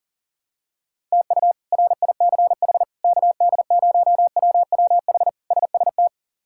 Morse code, heard as a keyed tone, about 40 wpm, 700 Hz. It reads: TU RICH KD0WWH SST
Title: K1USN SST QSO Practice - 40wpm